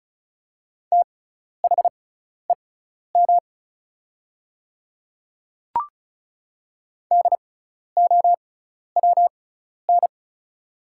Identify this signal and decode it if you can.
Morse code, heard as a keyed tone, about 36 wpm, spaced as for 12 wpm, 700 Hz. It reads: THEM DOWN
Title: 6x Character Spacing - Top 100 Words - 35wpm